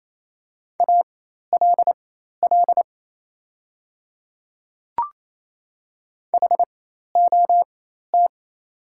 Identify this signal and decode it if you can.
Morse code, heard as a keyed tone, about 27 wpm, spaced as for 13 wpm, 700 Hz. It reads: ALL HOT
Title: 4x Character Spacing - Top 200 Words - 28wpm